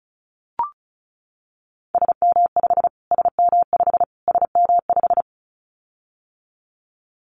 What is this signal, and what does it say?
Morse code, heard as a keyed tone, about 34 wpm, 700 Hz. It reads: SM5 SM5 SM5
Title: Callsign Prefixes for Top 25 DXCC - Repeated 3x - 35wpm